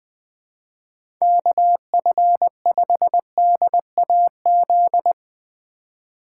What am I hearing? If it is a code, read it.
Morse code, heard as a keyed tone, about 20 wpm, 700 Hz. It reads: KF5DAZ
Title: Speed Racing - USA 2x3 Call Signs 17wpm